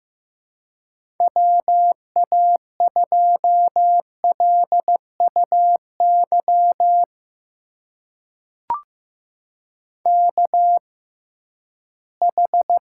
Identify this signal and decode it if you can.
Morse code, heard as a keyed tone, about 15 wpm, spaced as for 11 wpm, 700 Hz. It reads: WA2LUY KH
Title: USA Call Signs - 6 Characters - 6x Character Spacing - 15wpm